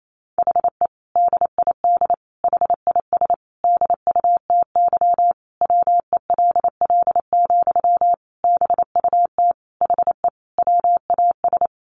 Morse code, heard as a keyed tone, about 28 wpm, 700 Hz. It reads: HE DID HIS DUTY WELL, BUT HE WAS
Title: Sentences from Top 1500 Words 28wpm